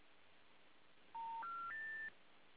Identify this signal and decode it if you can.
Special information tone.